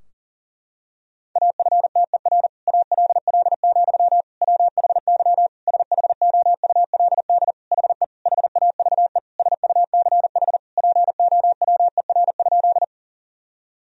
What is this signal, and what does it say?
Morse code, heard as a keyed tone, about 40 wpm, 700 Hz. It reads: AFTER ALL, WHY SHOULD HE HAVE SUCH POWER?
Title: Sentences from Top 1000 Words - Review Part 2 - 40wpm